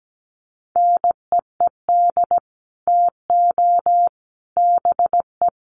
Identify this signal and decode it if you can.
Morse code, heard as a keyed tone, about 17 wpm, 700 Hz. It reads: NEED TO BE
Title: Sets of 5 Words Encore 17wpm